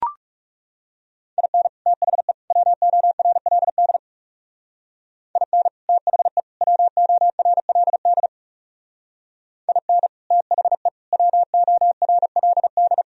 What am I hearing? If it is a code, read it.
Morse code, heard as a keyed tone, about 36 wpm, 700 Hz. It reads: IN THE WORLD IN THE WORLD IN THE WORLD
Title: Speed-Racing - Sets of 3 Words 30wpm